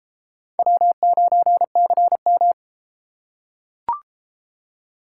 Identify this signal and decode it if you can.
Morse code, heard as a keyed tone, about 30 wpm, 700 Hz. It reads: W9CM